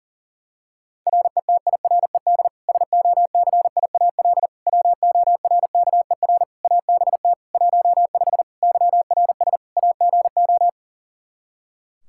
Morse code, heard as a keyed tone, about 40 wpm, 700 Hz. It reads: RETIRED SOCIAL WORKER ABT 15 YRS AGO
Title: K1USN SST QSO Practice 40wpm